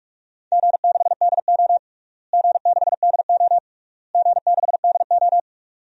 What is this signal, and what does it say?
Morse code, heard as a keyed tone, about 41 wpm, 700 Hz. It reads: G6DO G6DO G6DO
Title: DXCC-100 Call Signs - 4 Characters - Repeated 3x - 45wpm